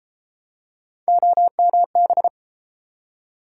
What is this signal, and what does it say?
Morse code, heard as a keyed tone, about 31 wpm, 700 Hz. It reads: OMB